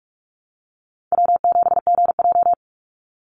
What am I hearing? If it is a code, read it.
Morse code, heard as a keyed tone, about 44 wpm, 700 Hz. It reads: W7GJ
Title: NA SOTA Chaser Callsigns 45wpm